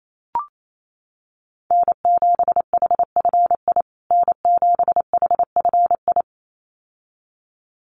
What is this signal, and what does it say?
Morse code, heard as a keyed tone, about 28 wpm, 700 Hz. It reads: N7HFI N7HFI
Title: USA Call Signs Repeated 2x 28wpm